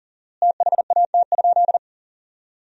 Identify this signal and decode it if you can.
Morse code, heard as a keyed tone, about 40 wpm, 700 Hz. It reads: THAT?
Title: 4x Word-Spacing - Sentences from Top 300 Words - 40wpm